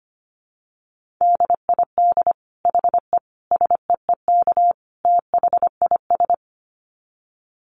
Morse code, heard as a keyed tone, about 25 wpm, 700 Hz. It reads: DID HE SEEK THIS
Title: Dit Challenge 25wpm